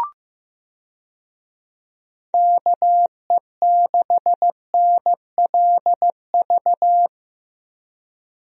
Morse code, heard as a keyed tone, about 15 wpm, 700 Hz. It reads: KE6NLV
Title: USA Call Signs - 6 Characters - Repeated 3x - 15wpm